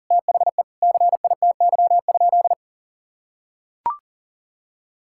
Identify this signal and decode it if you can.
Morse code, heard as a keyed tone, about 40 wpm, 700 Hz. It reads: THE CITY?
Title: Sentences from Top 1600 Words 40wpm